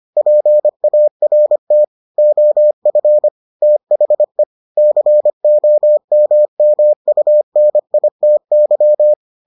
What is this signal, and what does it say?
Morse code, heard as a keyed tone, about 25 wpm, 600 Hz. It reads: PART OF THE COMMUNITY